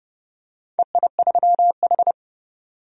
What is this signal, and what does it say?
Morse code, heard as a keyed tone, about 30 wpm, 700 Hz. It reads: EI3H
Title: DXCC-100 Call Signs - 4 Characters - 30wpm